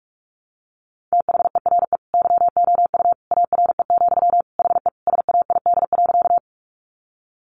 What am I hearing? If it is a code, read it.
Morse code, heard as a keyed tone, about 44 wpm, 700 Hz. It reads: THERE YOU ARE, HE SAID.
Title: Sentences from Top 100 Words 45wpm